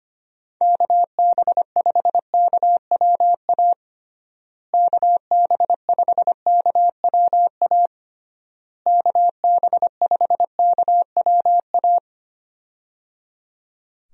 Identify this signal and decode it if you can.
Morse code, heard as a keyed tone, about 25 wpm, 700 Hz. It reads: KB5KWA KB5KWA KB5KWA
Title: USA Call Signs - 6 Characters - Repeated 3x - 25wpm